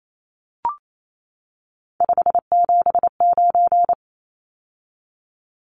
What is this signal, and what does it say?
Morse code, heard as a keyed tone, about 28 wpm, 700 Hz. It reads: 579
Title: Numbers - 3-Digit 28wpm